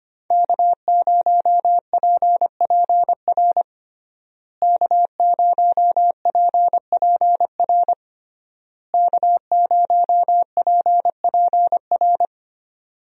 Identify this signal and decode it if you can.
Morse code, heard as a keyed tone, about 25 wpm, 700 Hz. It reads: K0PPR K0PPR K0PPR
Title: USA Call Signs - 5 Characters - Repeated 3x - 25wpm